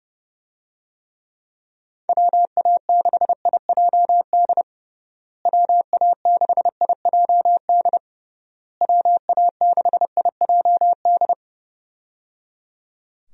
Morse code, heard as a keyed tone, about 30 wpm, 700 Hz. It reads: WA6IJD WA6IJD WA6IJD
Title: USA Call Signs - 6 Characters - Repeated 3x - 30wpm